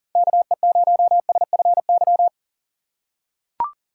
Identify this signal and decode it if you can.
Morse code, heard as a keyed tone, about 40 wpm, 700 Hz. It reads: KE0SFY
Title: USA Call Signs - 6 Characters - 2x Character Spacing - 40wpm